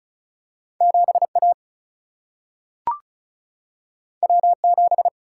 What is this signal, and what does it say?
Morse code, heard as a keyed tone, about 36 wpm, 700 Hz. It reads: 7 A W 7
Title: Single-Letter Single-Number 35wpm